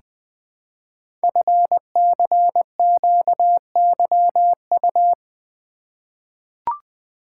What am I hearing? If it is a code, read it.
Morse code, heard as a keyed tone, about 20 wpm, 700 Hz. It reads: FCQYU
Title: Code Groups - 5 Letters - 20wpm